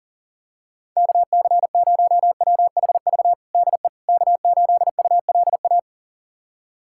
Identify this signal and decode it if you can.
Morse code, heard as a keyed tone, about 37 wpm, 700 Hz. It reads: KC0WHV DE K8ULA
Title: USA Callsign DE Callsign 40wpm